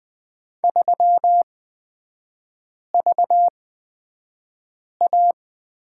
Morse code, heard as a keyed tone, about 20 wpm, spaced as for 5 wpm, 700 Hz. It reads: 3VA
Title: 255 Lesson 41e - Period - Three Characters 20wpm